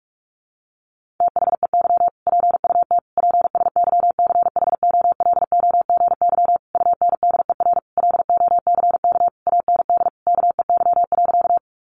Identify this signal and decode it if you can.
Morse code, heard as a keyed tone, about 44 wpm, 700 Hz. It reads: THEY PUT PSYCHOLOGY UNDER LOCK AND KEY.
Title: Sentences from Top 1400 Words 45wpm